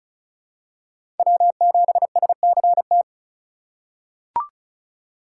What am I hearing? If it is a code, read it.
Morse code, heard as a keyed tone, about 36 wpm, 700 Hz. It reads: W7SCT
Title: USA Call Signs - 5 Characters - 4x Character Spacing - 35wpm